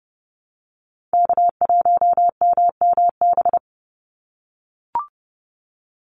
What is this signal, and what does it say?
Morse code, heard as a keyed tone, about 30 wpm, 700 Hz. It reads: K1MMB
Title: Call Signs - 30wpm